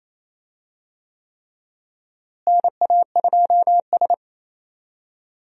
Morse code, heard as a keyed tone, about 28 wpm, 700 Hz. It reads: NA2S
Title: USA Call Signs - 4 Characters - Repeated 3x - 28wpm